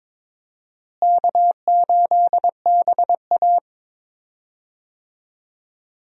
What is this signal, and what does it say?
Morse code, heard as a keyed tone, about 22 wpm, 700 Hz. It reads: K8BA
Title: USA Call Signs - 4 Characters - Repeated 3x - 22wpm